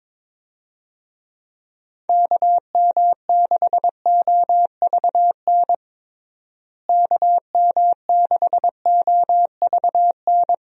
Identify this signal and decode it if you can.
Morse code, heard as a keyed tone, about 22 wpm, 700 Hz. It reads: KM6OVN KM6OVN
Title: USA Call Signs - 6 Characters - Repeated 2x - 22wpm